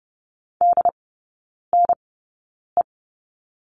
Morse code, heard as a keyed tone, about 30 wpm, spaced as for 9 wpm, 700 Hz. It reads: DNE
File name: Lesson 18e - Letter m - Three Characters 30wpm